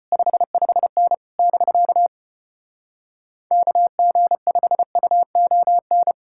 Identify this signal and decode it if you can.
Morse code, heard as a keyed tone, about 33 wpm, 700 Hz. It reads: 55N <BK> KG5UON